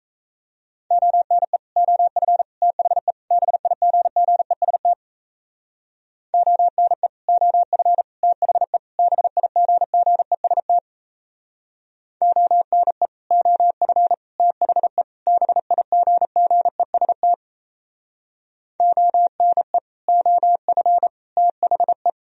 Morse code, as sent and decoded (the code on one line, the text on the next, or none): --- -. / --- ..-. / - .... / -... .. --. --. ... - / --- -. / --- ..-. / - .... / -... .. --. --. ... - / --- -. / --- ..-. / - .... / -... .. --. --. ... - / --- -. / --- ..-. / - .... .
ONE OF THE BIGGEST ONE OF THE BIGGEST ONE OF THE BIGGEST ONE OF THE